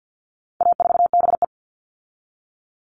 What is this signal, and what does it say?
Morse code, heard as a keyed tone, about 50 wpm, 700 Hz. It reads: A4BE
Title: All Cut Numbers - 4-Digit 50wpm